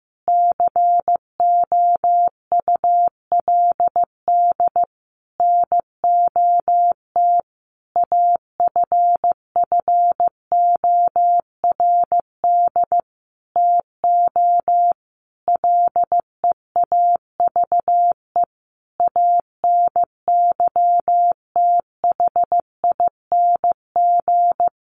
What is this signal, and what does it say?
Morse code, heard as a keyed tone, about 15 wpm, 700 Hz. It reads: COULD NOT AFFORD TO LEAVE ANYTHING
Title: Sentences from Top 1100 Words 15wpm